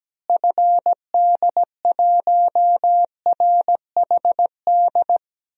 Morse code, heard as a keyed tone, about 17 wpm, 700 Hz. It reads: FD1RHD
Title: DXCC-100 Call Signs - 6 Characters - Repeated 2x - 17wpm